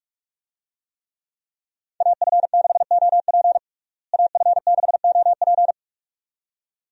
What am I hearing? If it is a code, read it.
Morse code, heard as a keyed tone, about 47 wpm, 700 Hz. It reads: AF6OP AF6OP
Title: USA Call Signs - 5 Characters - Repeated 2x - 45wpm